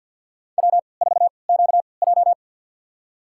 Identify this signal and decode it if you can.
Morse code, heard as a keyed tone, about 52 wpm, spaced as for 28 wpm, 700 Hz. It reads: W4QJ
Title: USA Call Signs - 3x Character Spacing - 50wpm